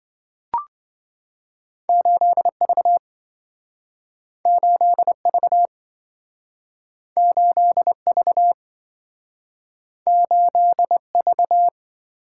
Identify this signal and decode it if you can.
Morse code, heard as a keyed tone, about 27 wpm, 700 Hz. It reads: 8V 8V 8V 8V